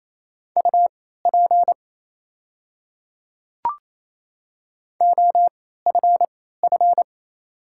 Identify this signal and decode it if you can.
Morse code, heard as a keyed tone, about 28 wpm, spaced as for 16 wpm, 700 Hz. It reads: UP OFF
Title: 3x Character Spacing - Top 200 Words - 28wpm